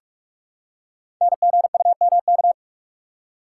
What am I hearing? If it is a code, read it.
Morse code, heard as a keyed tone, about 44 wpm, 690 Hz. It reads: NGUMK
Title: Code Groups - 5 Letters - 45wpm